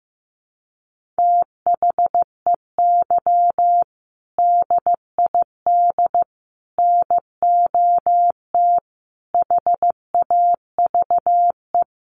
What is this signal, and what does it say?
Morse code, heard as a keyed tone, about 14 wpm, 700 Hz. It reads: THEY DID NOT HAVE
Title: Sentences from Top 1400 Words 15wpm